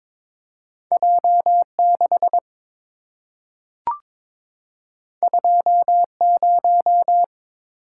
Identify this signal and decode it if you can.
Morse code, heard as a keyed tone, about 22 wpm, 700 Hz. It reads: J6 20
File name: practice / 22wpm / Two Letter-Number Combinations - 22wpm